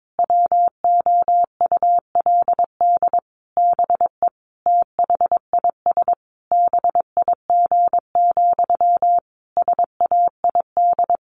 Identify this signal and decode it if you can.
Morse code, heard as a keyed tone, about 22 wpm, 700 Hz. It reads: WOULD BE THIS BIG, SAID
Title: A Sound Of Thunder 22wpm